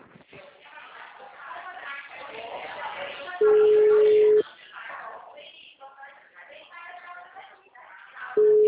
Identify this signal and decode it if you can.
Ringback tone (Europe, 425 Hz), with speech.
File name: out- -2004- - - .11657